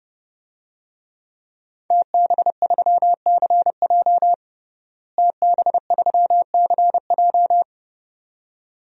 Morse code, heard as a keyed tone, about 30 wpm, 700 Hz. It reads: TB3CJ TB3CJ
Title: DXCC-100 Call Signs - 5 Characters - Repeated 2x - 30wpm